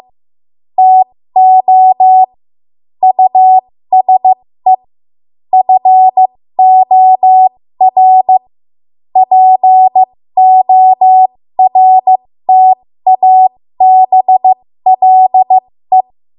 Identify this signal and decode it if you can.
Morse code, heard as a keyed tone, about 15 wpm, spaced as for 13 wpm, 750 Hz. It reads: TO USE FOR PORTABLE
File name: Morse / 13WPM